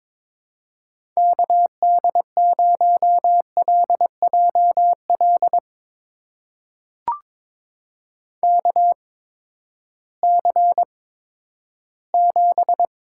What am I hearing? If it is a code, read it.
Morse code, heard as a keyed tone, about 22 wpm, spaced as for 12 wpm, 700 Hz. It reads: KD0LJL KC7
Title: USA Call Signs - 6 Characters - 8x Character Spacing - 22wpm